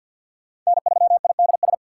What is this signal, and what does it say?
Morse code, heard as a keyed tone, about 48 wpm, 700 Hz. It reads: N3IDS
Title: USA Call Signs - 5 Characters - 6x Character Spacing - 50wpm